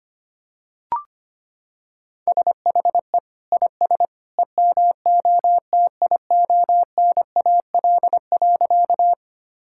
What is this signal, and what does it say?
Morse code, heard as a keyed tone, about 25 wpm, 700 Hz. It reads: SHE IS EMOTIONAL.
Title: Sentences from Top 1000 Words - Review Part 5 - 25wpm